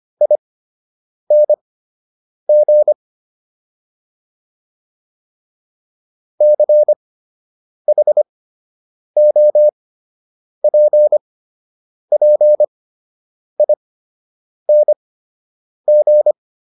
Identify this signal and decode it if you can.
Morse code, heard as a keyed tone, about 25 wpm, spaced as for 8 wpm, 600 Hz. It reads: ING CHOPPING